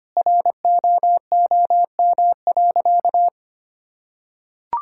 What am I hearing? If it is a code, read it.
Morse code, heard as a keyed tone, about 25 wpm, 700 Hz. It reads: ROOM.
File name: practice / 25wpm / Sentences with 2-Word Phrases - 25wpm